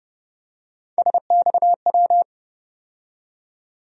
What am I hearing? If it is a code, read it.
Morse code, heard as a keyed tone, about 32 wpm, 700 Hz. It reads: SXW
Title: USA Call Sign Postfixes - 3 Letters 30wpm